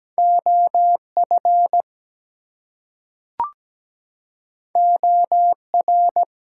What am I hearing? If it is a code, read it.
Morse code, heard as a keyed tone, about 17 wpm, 700 Hz. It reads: OF OR
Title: Two Letter Words - 17wpm